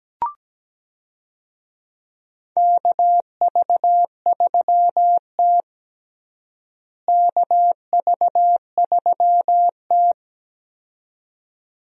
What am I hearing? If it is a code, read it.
Morse code, heard as a keyed tone, about 17 wpm, 700 Hz. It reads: KV3T KV3T
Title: USA Call Signs - 4 Characters - Repeated 2x - 17wpm